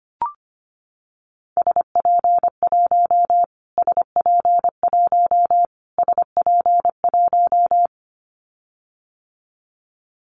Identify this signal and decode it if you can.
Morse code, heard as a keyed tone, about 25 wpm, 700 Hz. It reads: SP1 SP1 SP1
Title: Callsign Prefixes for Top 25 DXCC - Repeated 3x - 25wpm